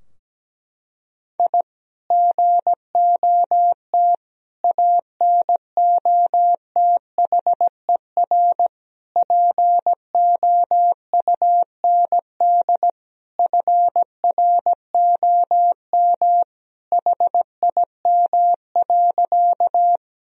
Morse code, heard as a keyed tone, about 17 wpm, 700 Hz. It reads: I GOT ANOTHER POUND FROM HIM.
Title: Sentences from Top 1000 Words - Review Part 2 - 17wpm